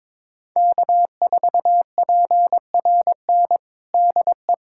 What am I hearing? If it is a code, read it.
Morse code, heard as a keyed tone, about 21 wpm, 700 Hz. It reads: K4PRN DE
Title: USA Callsign DE Callsign - 5 Characters - 22wpm